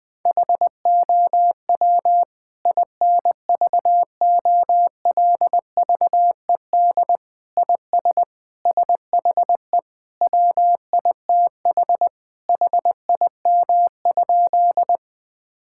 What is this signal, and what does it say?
Morse code, heard as a keyed tone, about 20 wpm, 700 Hz. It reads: HOW INVOLVED IS SHE WITH HIM?
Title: Sentences from Top 1000 Words - Review Part 5 - 20wpm